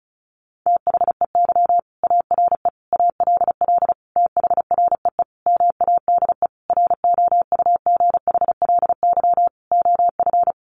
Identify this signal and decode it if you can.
Morse code, heard as a keyed tone, about 34 wpm, 700 Hz. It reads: THEY ARE ALL THREE MADE ROUGHLY OF
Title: Sentences from Top 1500 Words 35wpm